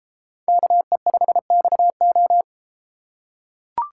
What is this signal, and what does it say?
Morse code, heard as a keyed tone, about 33 wpm, 700 Hz. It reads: KE5XO